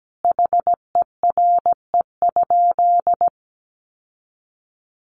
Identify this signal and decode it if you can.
Morse code, heard as a keyed tone, about 17 wpm, 700 Hz. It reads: HERE?
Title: Sentences from Top 1400 Words 17wpm